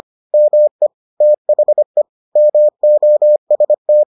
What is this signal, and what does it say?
Morse code, heard as a keyed tone, about 25 wpm, 600 Hz. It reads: ME THE MOST